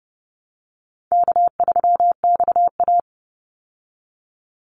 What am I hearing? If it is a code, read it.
Morse code, heard as a keyed tone, about 30 wpm, 700 Hz. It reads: K3XA
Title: Call Signs Repeated 2x - 30wpm